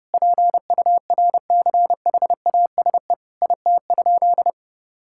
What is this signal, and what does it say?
Morse code, heard as a keyed tone, about 30 wpm, 700 Hz. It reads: PURCHASE IT?
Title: Sentences from Top 1000 Words - Review Part 5 - 30wpm